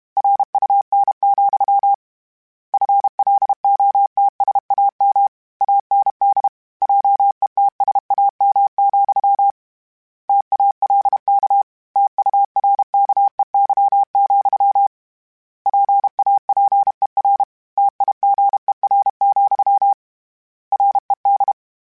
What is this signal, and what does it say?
Morse code, heard as a keyed tone, about 32 wpm, 800 Hz. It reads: RUN, FLOTSAM AND JETSAM, TALK TURKEY, PAPER TIGER, RED
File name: Misc / Misc-mp3s / Word Pairs @30